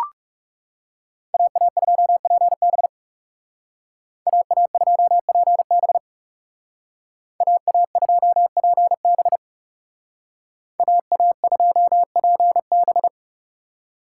Morse code, heard as a keyed tone, about 35 wpm, 700 Hz. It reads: AA2PB AA2PB AA2PB AA2PB
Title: Speed Racing - USA 2x2 Call Signs 30wpm